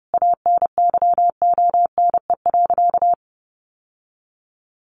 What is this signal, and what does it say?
Morse code, heard as a keyed tone, about 30 wpm, 700 Hz. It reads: ANYONE.
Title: Sentences from Top 900 Words 30wpm